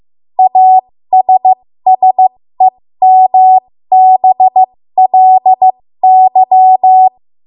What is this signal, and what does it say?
Morse code, heard as a keyed tone, about 15 wpm, spaced as for 13 wpm, 750 Hz. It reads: ASSEMBLY
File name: Morse / 13WPM